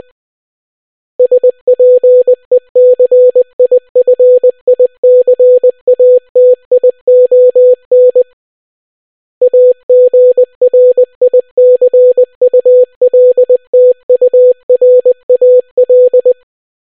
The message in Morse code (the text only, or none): SPECIFICATION AGRICULTURAL